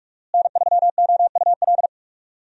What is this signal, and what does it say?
Morse code, heard as a keyed tone, about 43 wpm, 700 Hz. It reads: N3OUL
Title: USA Call Signs - 5 Characters - Repeated 3x - 45wpm